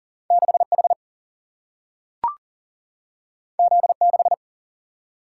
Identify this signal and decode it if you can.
Morse code, heard as a keyed tone, about 40 wpm, 700 Hz. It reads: 6 H Z 6
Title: Single-Letter Single-Number 40wpm